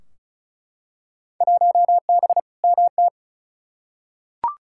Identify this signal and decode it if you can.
Morse code, heard as a keyed tone, about 36 wpm, 700 Hz. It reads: 1B MT
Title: Field Day Exchanges - 35wpm